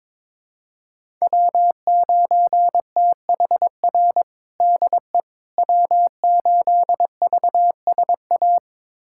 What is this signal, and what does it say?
Morse code, heard as a keyed tone, about 22 wpm, 700 Hz. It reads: W9THR DE W8VSA
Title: USA Callsign DE Callsign - 5 Characters - 22wpm